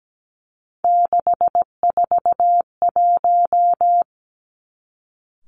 Morse code, heard as a keyed tone, about 17 wpm, 700 Hz. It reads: 641
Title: Three Challenging Letter-Number Combinations 17wpm